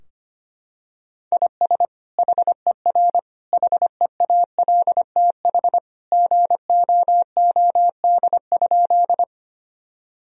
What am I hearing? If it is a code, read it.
Morse code, heard as a keyed tone, about 25 wpm, 700 Hz. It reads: IS HER HEALTH GOOD?
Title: Sentences from Top 1000 Words - Review Part 3 - 25wpm